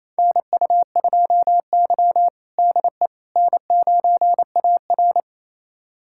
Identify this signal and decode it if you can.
Morse code, heard as a keyed tone, about 28 wpm, 700 Hz. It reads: NU2Y DE N9AR
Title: USA Callsign DE Callsign - 4 Characters - 28wpm